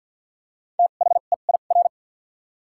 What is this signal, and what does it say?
Morse code, heard as a keyed tone, about 53 wpm, spaced as for 37 wpm, 690 Hz. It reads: THEIR
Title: 2x Character Spacing - Top 100 Words - 50wpm